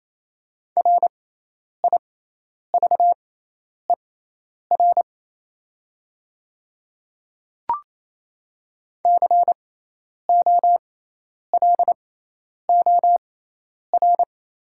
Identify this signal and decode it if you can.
Morse code, heard as a keyed tone, about 28 wpm, spaced as for 10 wpm, 700 Hz. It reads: RIVER COLOR
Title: 6x Character Spacing - Top 500 Words - 28wpm